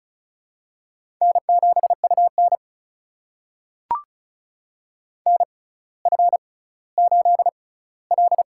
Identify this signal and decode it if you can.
Morse code, heard as a keyed tone, about 36 wpm, spaced as for 18 wpm, 700 Hz. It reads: N7UN NF8L
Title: USA Call Signs - 6x Character Spacing - 35wpm